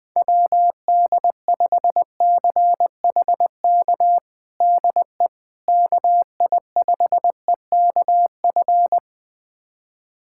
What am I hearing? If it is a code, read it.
Morse code, heard as a keyed tone, about 20 wpm, 700 Hz. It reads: WD5CHK DE KI5EKF
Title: USA Callsign DE Callsign 20wpm